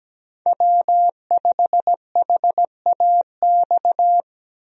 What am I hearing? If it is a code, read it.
Morse code, heard as a keyed tone, about 17 wpm, 700 Hz. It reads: W5HAX
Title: USA Call Signs - 5 Characters - Repeated 3x - 17wpm